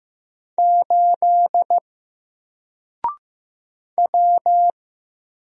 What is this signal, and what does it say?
Morse code, heard as a keyed tone, about 16 wpm, 700 Hz. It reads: 8 W
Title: Single Letter-Number 15wpm